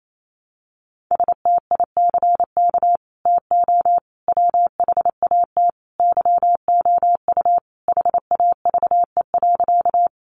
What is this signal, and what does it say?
Morse code, heard as a keyed tone, about 28 wpm, 700 Hz. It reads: STICK TO WHAT YOU HAVE.
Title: Sentences from Top 1200 Words 28wpm